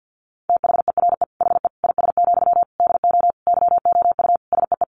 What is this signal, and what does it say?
Morse code, heard as a keyed tone, about 50 wpm, 700 Hz. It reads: THERE HE IS, DO YOU SEE
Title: Sentences from Top 100 Words 50wpm